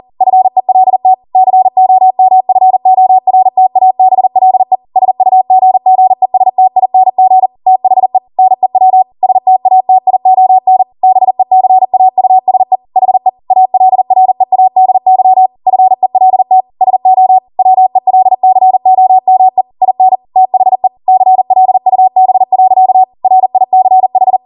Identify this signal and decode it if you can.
Morse code, heard as a keyed tone, about 40 wpm, 750 Hz. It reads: FELT COMFORTABLE SUGGESTING THE NEW STATION BECAUSE HE ALREADY FELT SO WELCOME IN THE CLUB. RICH